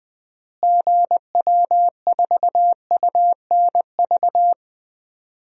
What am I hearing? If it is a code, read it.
Morse code, heard as a keyed tone, about 21 wpm, 700 Hz. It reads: GW4UNV
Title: DXCC-100 Call Signs - 6 Characters - Repeated 3x - 20wpm